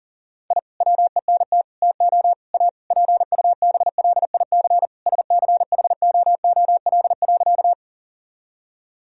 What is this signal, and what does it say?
Morse code, heard as a keyed tone, about 40 wpm, 700 Hz. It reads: I WENT TO A PUBLIC SCHOOL.
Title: Sentences from Top 1000 Words - Review Part 3 - 40wpm